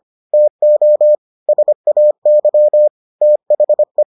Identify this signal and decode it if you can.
Morse code, heard as a keyed tone, about 25 wpm, 600 Hz. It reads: TO SAY THE